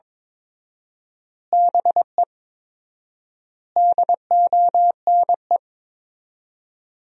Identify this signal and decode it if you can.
Morse code, heard as a keyed tone, about 22 wpm, 700 Hz. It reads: BE DONE
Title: 4x Word-Spacing - Sentences from Top 500 Words - 22wpm